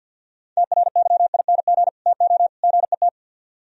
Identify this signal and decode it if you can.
Morse code, heard as a keyed tone, about 61 wpm, 700 Hz. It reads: TRYING TO GET
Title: Speed-Racing - Sets of 3 Words 50wpm